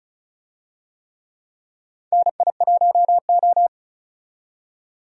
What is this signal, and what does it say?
Morse code, heard as a keyed tone, about 34 wpm, 700 Hz. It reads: NI1O
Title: USA Call Signs - 4 Characters - 35wpm